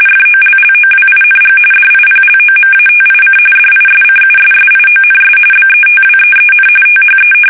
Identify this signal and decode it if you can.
RTTY, about 45.45 baud. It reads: FOLLOWING DIPLOMATIC DISCUSSIONS BETWEEN THE
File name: fsk850 / usnavy5